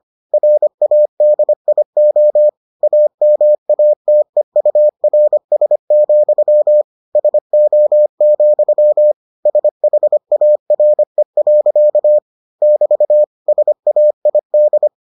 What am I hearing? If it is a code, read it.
Morse code, heard as a keyed tone, about 25 wpm, 600 Hz. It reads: RADIO AMATEURS, SO, SHARE. <BT> SAID